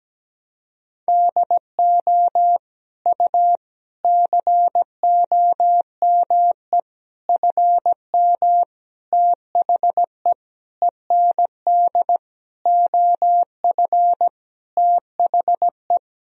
Morse code, heard as a keyed tone, about 17 wpm, 700 Hz. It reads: DO U COME FM THE END OF THE
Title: Sentences from Top 200 Words and Abbreviations 17wpm